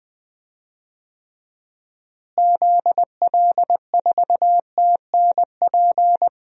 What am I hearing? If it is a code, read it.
Morse code, heard as a keyed tone, about 20 wpm, 700 Hz. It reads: ZL4TNP
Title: DXCC-100 Call Signs - 6 Characters - Repeated 3x - 20wpm